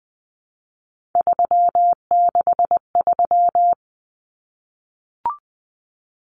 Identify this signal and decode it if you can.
Morse code, heard as a keyed tone, about 20 wpm, 700 Hz. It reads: 363
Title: Common Cut Numbers - 3-Digit 20wpm